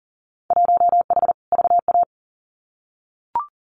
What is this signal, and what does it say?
Morse code, heard as a keyed tone, about 38 wpm, 700 Hz. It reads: 1H VA
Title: Winter Field Day Exchanges 40wpm